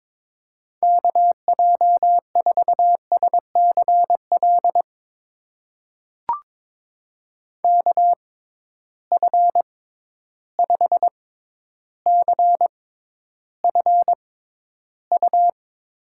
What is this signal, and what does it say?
Morse code, heard as a keyed tone, about 22 wpm, spaced as for 11 wpm, 700 Hz. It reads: KJ4SCL KF5CFU
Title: USA Call Signs - 6 Characters - 6x Character Spacing - 22wpm